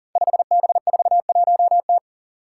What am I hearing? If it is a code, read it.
Morse code, heard as a keyed tone, about 37 wpm, 700 Hz. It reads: 5B41T
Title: Five Challenging Letter-Number Combinations 40wpm